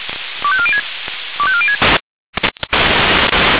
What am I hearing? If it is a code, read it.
Five-tone selective call (ZVEI), heard as quick runs of five tones, tones of about 70 ms. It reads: 25406 25406